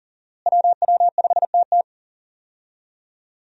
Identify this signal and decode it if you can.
Morse code, heard as a keyed tone, about 38 wpm, 700 Hz. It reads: WW5TT
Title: USA Call Signs - 5 Characters - Repeated 3x - 40wpm